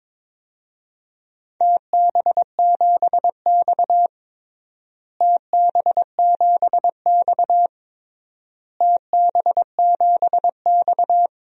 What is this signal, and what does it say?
Morse code, heard as a keyed tone, about 22 wpm, 700 Hz. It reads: TB7X TB7X TB7X
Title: DXCC-100 Call Signs - 4 Characters - Repeated 3x - 22wpm